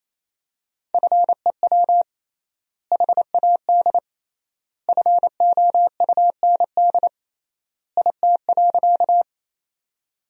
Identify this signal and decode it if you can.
Morse code, heard as a keyed tone, about 28 wpm, 700 Hz. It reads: FEW HAD FOUND IT.
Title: 3x Word-Spacing - Sentences from Top 300 Words - 28wpm